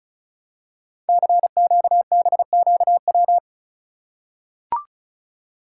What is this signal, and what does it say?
Morse code, heard as a keyed tone, about 36 wpm, 700 Hz. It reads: CQBQW
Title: Code Groups - 5 Letter:Numbers - 35wpm